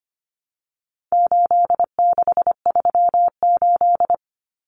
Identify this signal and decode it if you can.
Morse code, heard as a keyed tone, about 25 wpm, 700 Hz. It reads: 8638
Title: Common Cut Numbers - 4-Digit 25wpm